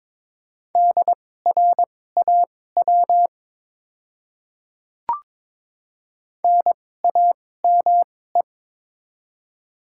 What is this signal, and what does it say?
Morse code, heard as a keyed tone, about 22 wpm, spaced as for 16 wpm, 690 Hz. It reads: DRAW NAME
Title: 2x Character Spacing - Top 300 Words - 22wpm